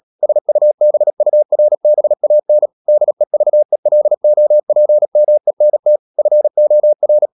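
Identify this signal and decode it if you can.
Morse code, heard as a keyed tone, about 37 wpm, 600 Hz. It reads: SUBURBAN DEVELOPMENT FOR
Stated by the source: Morse code practice at 35–39 wpm sending English text of Ken Follett's novel Never, from CW med Gnister.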